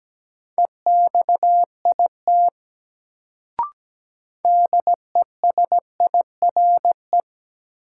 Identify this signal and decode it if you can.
Morse code, heard as a keyed tone, about 17 wpm, 700 Hz. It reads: EXIT DESIRE
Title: Sentences from Top 1100 Words - New Word Drill - 17wpm